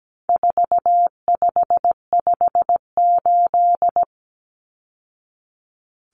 Morse code, heard as a keyed tone, about 17 wpm, 700 Hz. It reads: 4558
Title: Numbers - 4-Digit 17wpm